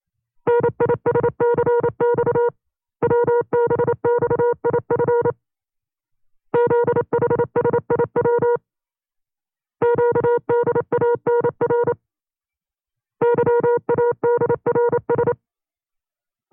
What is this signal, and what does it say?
Morse code, heard as a keyed tone, about 29 wpm, 480 Hz. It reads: NISCX WBXIF ZHSIW QDANR YADRS
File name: CW-practice/25WPM